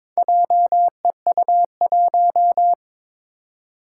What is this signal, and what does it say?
Morse code, heard as a keyed tone, about 22 wpm, 700 Hz. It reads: JEU1
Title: Four Challenging Letter-Number Combinations 22wpm